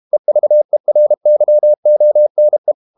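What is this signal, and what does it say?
Morse code, heard as a keyed tone, about 32 wpm, 600 Hz. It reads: EVERYONE
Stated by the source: Morse code practice at 32 wpm from CW med Gnister, reading Ken Follett's novel Never in English.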